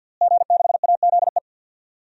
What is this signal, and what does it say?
Morse code, heard as a keyed tone, about 47 wpm, 700 Hz. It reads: G6AZE
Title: DXCC-100 Call Signs - 5 Characters - Repeated 2x - 50wpm